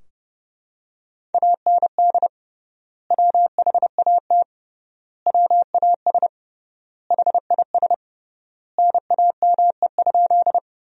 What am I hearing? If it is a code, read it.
Morse code, heard as a keyed tone, about 30 wpm, 700 Hz. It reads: AND WHAT WAS HIS NAME?
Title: 3x Word-Spacing - Sentences from Top 200 Words - 30wpm